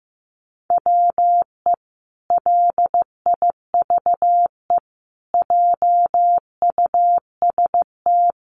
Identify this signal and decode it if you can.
Morse code, heard as a keyed tone, about 15 wpm, 700 Hz. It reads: WE LIVE JUST
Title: Sentences from Top 1300 Words 15wpm